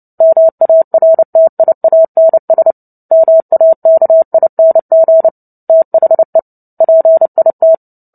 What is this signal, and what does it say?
Morse code, heard as a keyed tone, about 30 wpm, 650 Hz. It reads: MARTIANS MAKING THE PIT
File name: Book Two / 03-Chapter 3 - 29 WPM